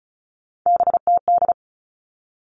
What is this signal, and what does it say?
Morse code, heard as a keyed tone, about 34 wpm, 700 Hz. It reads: BTD
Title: All Cut Numbers - 3-Digit 35wpm